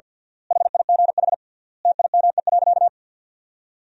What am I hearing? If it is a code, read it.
Morse code, heard as a keyed tone, about 53 wpm, 700 Hz. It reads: HIGH TIME.
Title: 3x Word-Spacing - Sentences from Top 200 Words - 50wpm